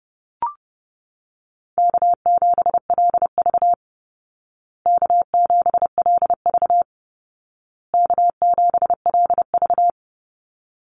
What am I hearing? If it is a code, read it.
Morse code, heard as a keyed tone, about 30 wpm, 700 Hz. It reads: K7LV K7LV K7LV
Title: Call Signs Repeated 2x - 30wpm